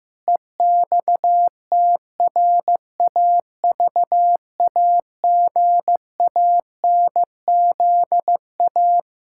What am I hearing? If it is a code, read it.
Morse code, heard as a keyed tone, about 15 wpm, 700 Hz. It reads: EXTRAVAGANZA
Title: Pangram Challenge v2 - 15wpm